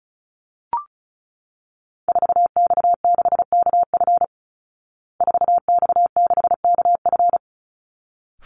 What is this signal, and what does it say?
Morse code, heard as a keyed tone, about 34 wpm, 700 Hz. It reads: 4X6KF 4X6KF
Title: Call Signs Repeated 3x - 35wpm